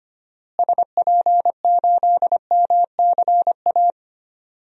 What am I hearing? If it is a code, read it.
Morse code, heard as a keyed tone, about 25 wpm, 700 Hz. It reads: SP8MCA
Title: DXCC-100 Call Signs - 6 Characters - Repeated 3x - 25wpm